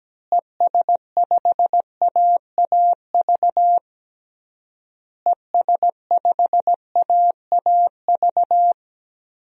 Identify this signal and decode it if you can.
Morse code, heard as a keyed tone, about 17 wpm, 700 Hz. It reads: ES5AAV ES5AAV
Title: DXCC-100 Call Signs - 6 Characters - Repeated 3x - 17wpm